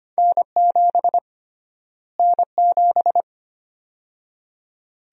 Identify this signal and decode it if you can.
Morse code, heard as a keyed tone, about 25 wpm, 700 Hz. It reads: N7 N7
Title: USA Call Sign Prefixes - Repeated 2x - 25wpm